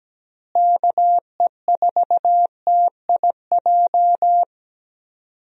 Morse code, heard as a keyed tone, about 17 wpm, 700 Hz. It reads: KE4TIJ
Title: USA Call Signs - 6 Characters - Repeated 2x - 17wpm